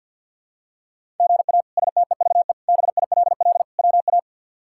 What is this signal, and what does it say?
Morse code, heard as a keyed tone, about 49 wpm, 700 Hz. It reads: GA STEVE BILL WA
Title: K1USN SST QSO Practice - 50wpm